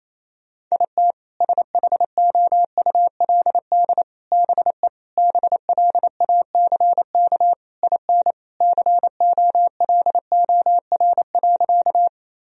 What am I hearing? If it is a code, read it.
Morse code, heard as a keyed tone, about 28 wpm, 700 Hz. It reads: IT SHOULD BE BLACK IN COLOR.
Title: Sentences from Top 1000 Words - Review Part 2 - 28wpm